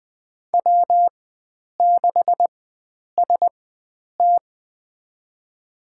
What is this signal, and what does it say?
Morse code, heard as a keyed tone, about 20 wpm, spaced as for 9 wpm, 700 Hz. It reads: W6ST
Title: USA Call Signs - 4 Characters - 4x Character Spacing - 20wpm